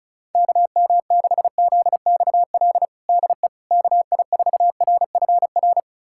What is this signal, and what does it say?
Morse code, heard as a keyed tone, about 36 wpm, 700 Hz. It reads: KM6ZXL DE KI4RFR
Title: USA Callsign DE Callsign 35wpm